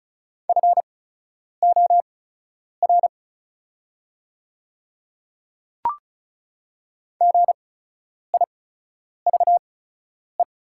Morse code, heard as a keyed tone, about 36 wpm, spaced as for 10 wpm, 700 Hz. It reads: FOR GIVE
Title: 8x Character Spacing - Top 300 Words - 35wpm